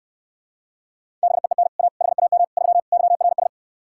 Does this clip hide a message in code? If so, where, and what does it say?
Yes: Morse code from 1.2 to 3.5 s, heard as a keyed tone, about 69 wpm, 700 Hz. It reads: BEEN A HAM 3 YRS